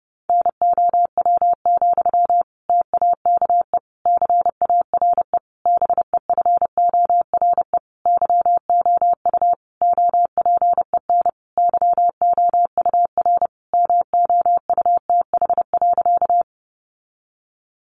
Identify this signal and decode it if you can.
Morse code, heard as a keyed tone, about 30 wpm, 700 Hz. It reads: NOW, TAKE CARE BEFORE YOU OPEN YOUR MOUTH.
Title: Sentences from Top 1500 Words 30wpm